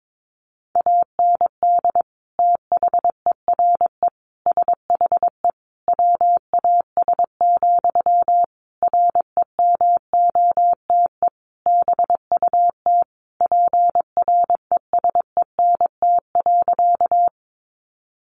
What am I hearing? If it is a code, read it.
Morse code, heard as a keyed tone, about 22 wpm, 700 Hz. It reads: AND THERE SHE WAS, REMOTE BUT PRESENT.
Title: Sentences from Top 1300 Words 22wpm